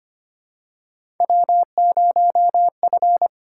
Morse code, heard as a keyed tone, about 25 wpm, 700 Hz. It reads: W0F
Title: USA Call Signs - 5 Characters - 6x Character Spacing - 25wpm